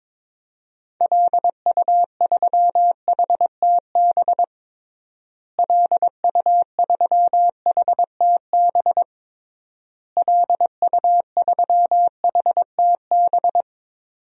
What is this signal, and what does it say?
Morse code, heard as a keyed tone, about 22 wpm, 700 Hz. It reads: LU3HTB LU3HTB LU3HTB
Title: DXCC-100 Call Signs - 6 Characters - Repeated 3x - 22wpm